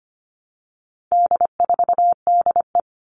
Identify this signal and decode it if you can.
Morse code, heard as a keyed tone, about 25 wpm, 700 Hz. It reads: D4DE
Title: All Cut Numbers - 4-Digit 25wpm